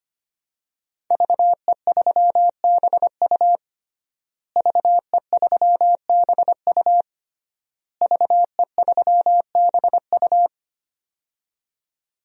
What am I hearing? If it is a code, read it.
Morse code, heard as a keyed tone, about 25 wpm, 700 Hz. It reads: VE3BU VE3BU VE3BU
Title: DXCC-100 Call Signs - 5 Characters - Repeated 3x - 25wpm